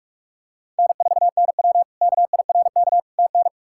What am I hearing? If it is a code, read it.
Morse code, heard as a keyed tone, about 48 wpm, 700 Hz. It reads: N4NW KIRK TN
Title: K1USN SST QSO Practice - 45wpm